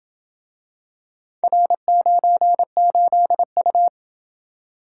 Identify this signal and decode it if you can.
Morse code, heard as a keyed tone, about 28 wpm, 700 Hz. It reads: R98U